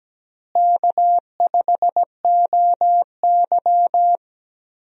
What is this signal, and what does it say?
Morse code, heard as a keyed tone, about 17 wpm, 700 Hz. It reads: K5OY
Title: USA Call Signs - 4 Characters - Repeated 2x - 17wpm